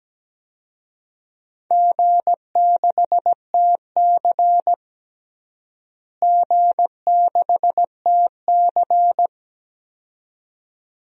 Morse code, heard as a keyed tone, about 17 wpm, 700 Hz. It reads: G6TC G6TC
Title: DXCC-100 Call Signs - 4 Characters - Repeated 2x - 17wpm